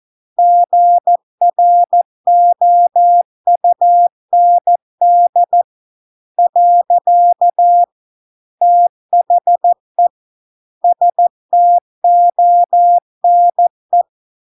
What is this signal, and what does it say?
Morse code, heard as a keyed tone, about 14 wpm, 700 Hz. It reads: GROUND . THE STONE